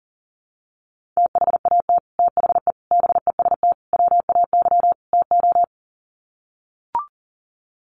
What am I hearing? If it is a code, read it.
Morse code, heard as a keyed tone, about 40 wpm, 700 Hz. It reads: THAT THE BEST WAY TO
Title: Sets of 5 Words Encore2 40wpm